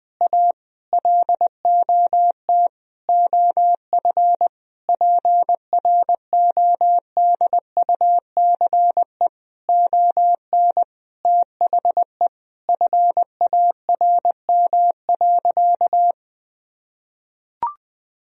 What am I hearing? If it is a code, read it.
Morse code, heard as a keyed tone, about 20 wpm, 700 Hz. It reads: A LOT OF PRODUCE ON THE FARM.
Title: Sentences from Top 1000 Words - Review Part 3 - 20wpm